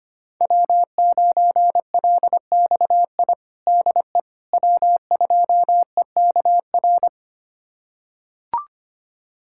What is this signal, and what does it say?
Morse code, heard as a keyed tone, about 25 wpm, 700 Hz. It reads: W9LXI DE W2EKR
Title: USA Callsign DE Callsign - 5 Characters - 25wpm